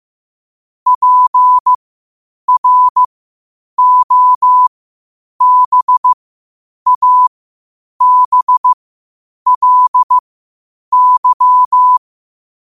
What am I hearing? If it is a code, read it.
Morse code, heard as a keyed tone, about 15 wpm, spaced as for 8 wpm, 1000 Hz. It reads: PROBABLY
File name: qotdfiles / 8wpm